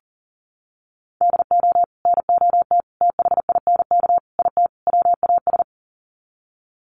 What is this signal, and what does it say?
Morse code, heard as a keyed tone, about 40 wpm, 700 Hz. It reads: DO NOT THINK IT WAS
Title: Sets of 5 Words Encore 40wpm